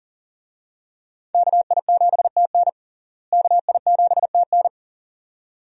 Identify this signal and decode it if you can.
Morse code, heard as a keyed tone, about 40 wpm, 700 Hz. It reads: KI7TN KI7TN
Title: USA Call Signs - 5 Characters - Repeated 2x - 40wpm